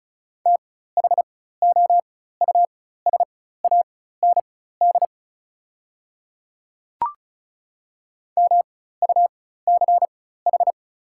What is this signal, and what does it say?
Morse code, heard as a keyed tone, about 36 wpm, spaced as for 16 wpm, 690 Hz. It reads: THOUSAND MUCH
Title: 4x Character Spacing - Top 500 Words - 35wpm